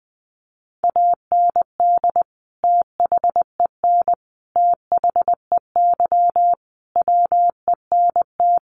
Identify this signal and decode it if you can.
Morse code, heard as a keyed tone, about 20 wpm, 700 Hz. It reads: AND THEN THEY WENT
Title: Sentences from Top 1400 Words 20wpm